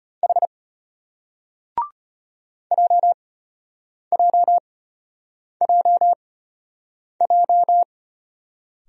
Morse code, heard as a keyed tone, about 33 wpm, spaced as for 8 wpm, 700 Hz. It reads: H JJJJ